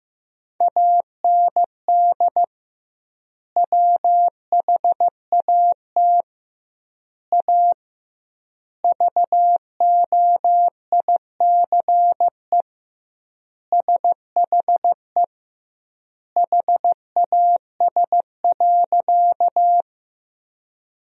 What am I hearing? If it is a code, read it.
Morse code, heard as a keyed tone, about 14 wpm, 700 Hz. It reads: AND WHAT A VOICE SHE HAS.
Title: 2x Word-Spacing - Sentences from Top 500 Words - 15wpm